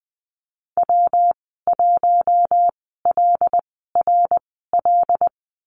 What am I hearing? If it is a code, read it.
Morse code, heard as a keyed tone, about 20 wpm, spaced as for 14 wpm, 700 Hz. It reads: W1LRL
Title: 122 Lesson 20g - Number 3 - Call Signs 20wpm